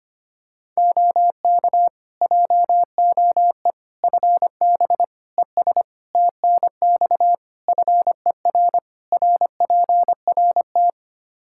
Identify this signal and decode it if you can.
Morse code, heard as a keyed tone, about 25 wpm, 700 Hz. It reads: OK JOE FB ES TNX FER RPRT
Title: K1USN SST QSO Practice 25wpm